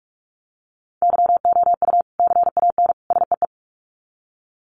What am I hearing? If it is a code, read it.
Morse code, heard as a keyed tone, about 44 wpm, 700 Hz. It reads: YOU CAN SEE
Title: Sets of 3 Words 45wpm